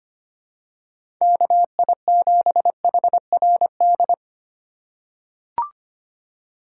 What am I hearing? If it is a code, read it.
Morse code, heard as a keyed tone, about 25 wpm, 700 Hz. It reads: KI7HRD
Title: USA Call Signs - 6 Characters - 3x Character Spacing - 25wpm